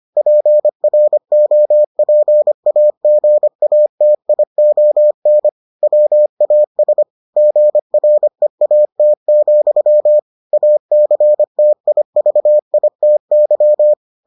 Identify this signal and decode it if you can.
Morse code, heard as a keyed tone, about 25 wpm, 600 Hz. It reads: PROPAGATION WAS GREAT, ACTIVITY